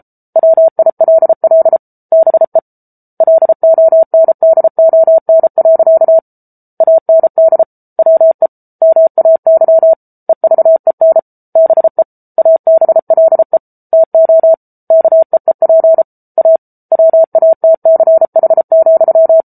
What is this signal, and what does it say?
Morse code, heard as a keyed tone, about 34 wpm, 650 Hz. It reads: WILL BE LONDON. AND WE MAY EVEN BE ABLE TO KEEP A WATCH,